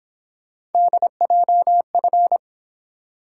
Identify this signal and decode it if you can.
Morse code, heard as a keyed tone, about 28 wpm, 700 Hz. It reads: DJF
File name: Speed-Racing - Three Letter-Number Combinations - 22wpm